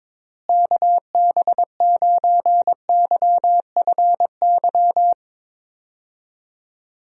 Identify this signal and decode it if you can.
Morse code, heard as a keyed tone, about 22 wpm, 700 Hz. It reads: KB9YFY
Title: USA Call Signs - 6 Characters - Repeated 3x - 22wpm